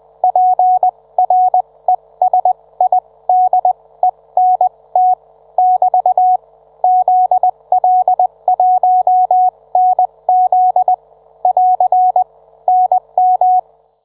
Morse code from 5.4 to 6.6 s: <BT>